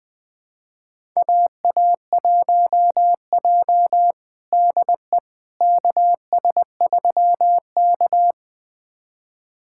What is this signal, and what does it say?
Morse code, heard as a keyed tone, about 20 wpm, 700 Hz. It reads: AA1J DE KS3K
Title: USA Callsign DE Callsign - 4 Characters - 20wpm